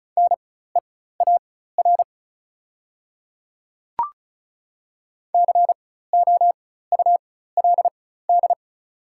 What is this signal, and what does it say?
Morse code, heard as a keyed tone, about 36 wpm, spaced as for 16 wpm, 700 Hz. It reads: NEAR COULD
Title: 4x Character Spacing - Top 200 Words - 35wpm